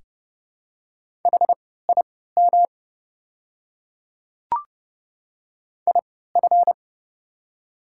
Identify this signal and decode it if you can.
Morse code, heard as a keyed tone, about 30 wpm, spaced as for 17 wpm, 700 Hz. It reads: HIM IF
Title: 3x Character Spacing - Top 100 Words - 30wpm